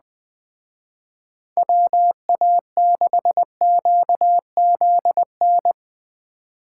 Morse code, heard as a keyed tone, about 20 wpm, 700 Hz. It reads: WA6QZN
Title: Speed Racing - USA 2x3 Call Signs 17wpm